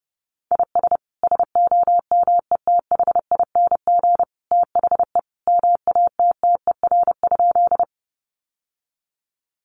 Morse code, heard as a keyed tone, about 30 wpm, 700 Hz. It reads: IS SOMETHING THE MATTER?
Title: Sentences from Top 700 Words 30wpm